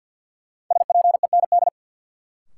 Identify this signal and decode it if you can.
Morse code, heard as a keyed tone, about 48 wpm, 700 Hz. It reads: SPEND
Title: Sets of 5 Words - New Word Drill - 50wpm